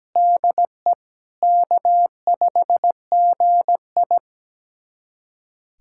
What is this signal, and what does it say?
Morse code, heard as a keyed tone, about 17 wpm, 700 Hz. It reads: DE K5GI
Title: USA Callsign DE Callsign - 4 Characters - 17wpm